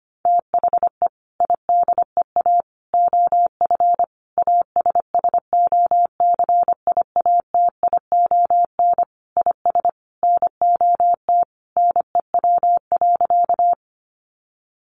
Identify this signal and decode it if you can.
Morse code, heard as a keyed tone, about 25 wpm, 700 Hz. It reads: THE IDEA OF ASSOCIATION IS NOT NEW.
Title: Sentences from Top 1100 Words 25wpm